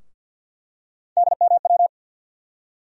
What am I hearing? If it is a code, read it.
Morse code, heard as a keyed tone, about 52 wpm, 700 Hz. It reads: DMW
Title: USA Call Sign Postfixes - 3 Letters 50wpm